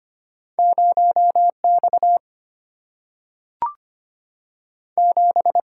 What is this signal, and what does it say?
Morse code, heard as a keyed tone, about 25 wpm, 700 Hz. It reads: 0X 7
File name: Two Letter-Number Combinations - 25wpm